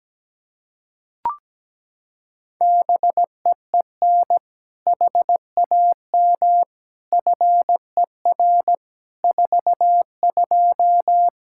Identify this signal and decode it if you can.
Morse code, heard as a keyed tone, about 17 wpm, 700 Hz. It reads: BEEN HAM FER 42
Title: Practice QSO - Exchange 3 - More Info - 17wpm